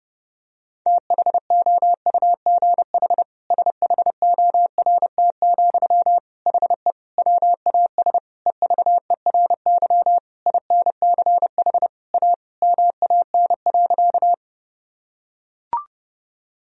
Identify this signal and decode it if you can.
Morse code, heard as a keyed tone, about 30 wpm, 700 Hz. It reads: THOUGH SHORT, HE WAS EVERY INCH A MAN.
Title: Sentences from Top 1000 Words - Review Part 3 - 30wpm